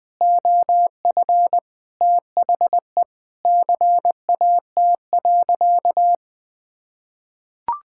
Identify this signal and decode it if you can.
Morse code, heard as a keyed tone, about 20 wpm, 700 Hz. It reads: OF THE CAT.
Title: Sentences with 3-Word Phrases - 20wpm